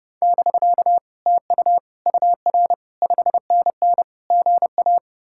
Morse code, heard as a keyed tone, about 40 wpm, 700 Hz. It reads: <BK> TU UR 5NN GA